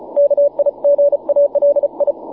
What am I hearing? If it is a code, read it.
Morse code, heard as a keyed tone, about 34 wpm, 570 Hz, with radio noise. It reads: KIGALI
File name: wp / 9x0cw 12m cw 14 feb 2010 1058z